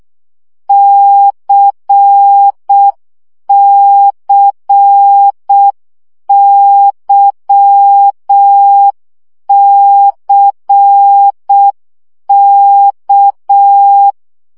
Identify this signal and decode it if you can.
Morse code, heard as a keyed tone, about 6 wpm, 800 Hz. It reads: CCYCK
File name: CW gyakorlas k c y 5 baud